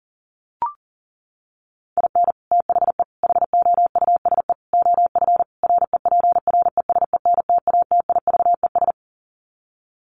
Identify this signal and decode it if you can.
Morse code, heard as a keyed tone, about 37 wpm, 700 Hz. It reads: IN THE HOUSE OF REPRESENTATIVES
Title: Sets of 5 Words Encore 40wpm